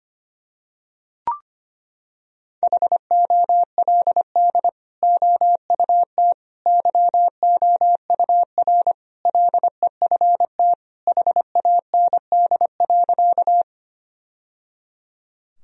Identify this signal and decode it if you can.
Morse code, heard as a keyed tone, about 25 wpm, 700 Hz. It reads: HOLD OUT YOUR LEFT HAND.
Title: Sentences from Top 1000 Words - Review Part 2 - 25wpm